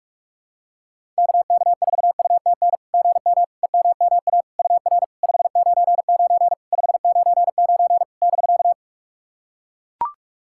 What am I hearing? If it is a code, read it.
Morse code, heard as a keyed tone, about 43 wpm, 690 Hz. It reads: KK4UTN GM EMMA UR 599 599 <BK>
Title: SOTA QSO Practice 45wpm